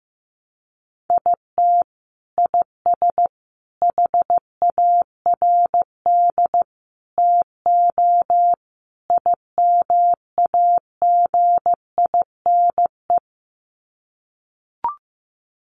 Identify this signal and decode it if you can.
Morse code, heard as a keyed tone, about 15 wpm, 700 Hz. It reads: IT IS HARD TO IMAGINE
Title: Sets of 5 Words Encore2 15wpm